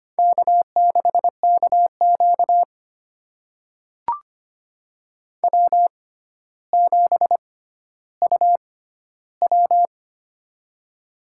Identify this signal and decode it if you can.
Morse code, heard as a keyed tone, about 25 wpm, spaced as for 13 wpm, 700 Hz. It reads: K6KQ W7UW
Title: USA Call Signs - 4 Characters - 6x Character Spacing - 25wpm